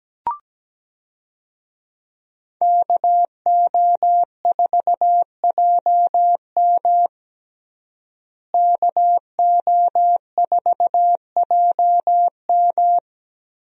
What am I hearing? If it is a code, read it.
Morse code, heard as a keyed tone, about 17 wpm, 700 Hz. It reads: KO4JM KO4JM
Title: USA Call Signs - 5 Characters - Repeated 2x - 17wpm